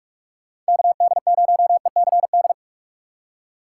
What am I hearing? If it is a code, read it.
Morse code, heard as a keyed tone, about 48 wpm, 700 Hz. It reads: KD0ECD
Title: Speed Racing - USA 2x3 Call Signs 45wpm